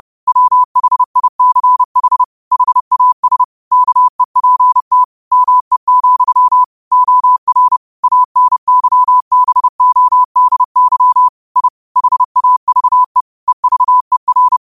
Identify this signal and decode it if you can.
Morse code, heard as a keyed tone, about 30 wpm, 1000 Hz. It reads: WHICH HAS KEPT ME, OR ANYBODY I HAVE EVER
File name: 30wpm